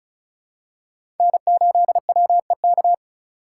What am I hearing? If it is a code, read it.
Morse code, heard as a keyed tone, about 36 wpm, 700 Hz. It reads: N8WEK
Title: USA Call Signs - 3x Character Spacing - 35wpm